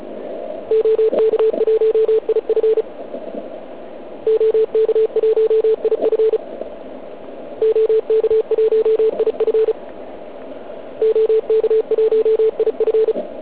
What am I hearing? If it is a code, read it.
Morse code, heard as a keyed tone, about 36 wpm, 440 Hz. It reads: OK1IF OK1IF OK1IF OK1IF